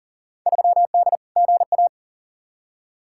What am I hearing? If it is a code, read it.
Morse code, heard as a keyed tone, about 37 wpm, 700 Hz. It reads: 3D GA
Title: Field Day Exchanges Encore - 40wpm